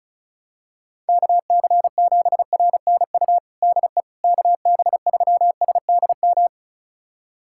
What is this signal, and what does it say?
Morse code, heard as a keyed tone, about 34 wpm, 700 Hz. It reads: KC7RNU DE KB3SDM
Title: USA Callsign DE Callsign - 6 Characters - 35wpm